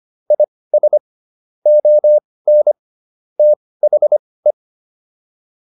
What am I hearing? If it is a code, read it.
Morse code, heard as a keyed tone, about 25 wpm, spaced as for 18 wpm, 600 Hz. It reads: IS ON THE